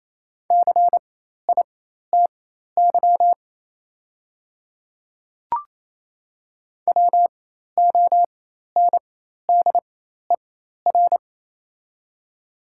Morse code, heard as a keyed tone, about 27 wpm, spaced as for 13 wpm, 690 Hz. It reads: CITY WONDER
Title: 4x Character Spacing - Top 500 Words - 28wpm